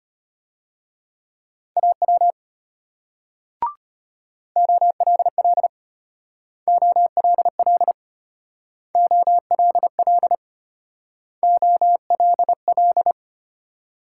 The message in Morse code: AW OLL OLL OLL OLL

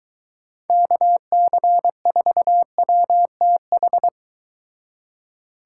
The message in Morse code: KC4WTH

